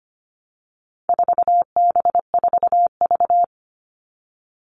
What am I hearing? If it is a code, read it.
Morse code, heard as a keyed tone, about 25 wpm, 700 Hz. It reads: 4B4V
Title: All Cut Numbers - 4-Digit 25wpm